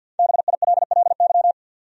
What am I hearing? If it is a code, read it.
Morse code, heard as a keyed tone, about 50 wpm, 700 Hz. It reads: BILLY
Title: USA Common Names - Max 5 Letters - 50wpm